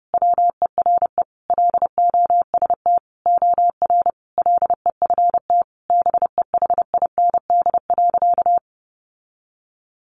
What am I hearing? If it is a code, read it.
Morse code, heard as a keyed tone, about 30 wpm, 700 Hz. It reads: WERE LOST OR LEFT BEHIND.
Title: Sentences from Top 1400 Words 30wpm